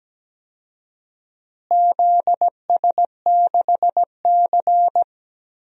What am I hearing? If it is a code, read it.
Morse code, heard as a keyed tone, about 17 wpm, 700 Hz. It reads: ZS6C
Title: DXCC-100 Call Signs - 4 Characters - Repeated 3x - 17wpm